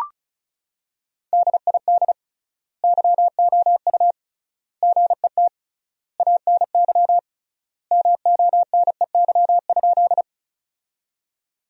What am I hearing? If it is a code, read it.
Morse code, heard as a keyed tone, about 36 wpm, 700 Hz. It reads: DID YOU GET ANY MONEY?
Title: 3x Word-Spacing - Sentences from Top 400 Words - 35wpm